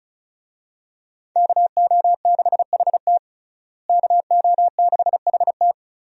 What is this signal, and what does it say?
Morse code, heard as a keyed tone, about 34 wpm, 700 Hz. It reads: KO6HT KO6HT
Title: USA Call Signs - 5 Characters - Repeated 2x - 35wpm